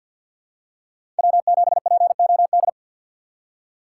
Morse code, heard as a keyed tone, about 47 wpm, 700 Hz. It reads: W7POD
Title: USA Call Signs - 5 Characters - 3x Character Spacing - 50wpm